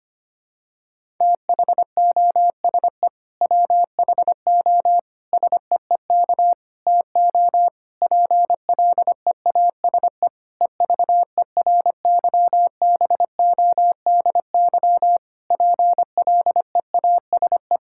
Morse code, heard as a keyed tone, about 25 wpm, 700 Hz. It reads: THOSE WHO SEEK TO PLEASE EVERYBODY PLEASE
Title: Sentences from Top 1600 Words 25wpm